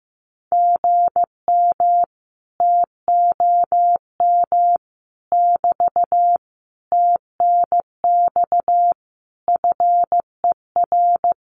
Morse code, heard as a keyed tone, about 15 wpm, 700 Hz. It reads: GM TOM <BT> TNX FER
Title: Common QSO Phrases Encore 15wpm